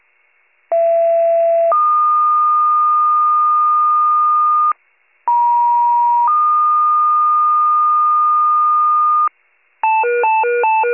Two-tone page, heard as a one-second tone, then a three-second tone, about 670 then 1220 Hz, about 950 then 1220 Hz. It is heard from 0.7 to 9.3 s.